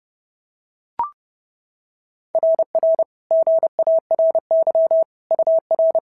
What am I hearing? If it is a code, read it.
Morse code, heard as a keyed tone, about 30 wpm, 650 Hz. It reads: RR GARY UR
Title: SOTA QSO Practice 30wpm